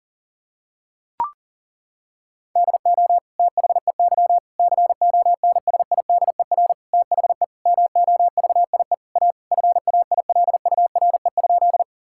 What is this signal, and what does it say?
Morse code, heard as a keyed tone, about 37 wpm, 700 Hz. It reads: DO THEY CONSIDER THE MOVIE A FAILURE?
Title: Sentences from Top 1000 Words - Review Part 6 - 40wpm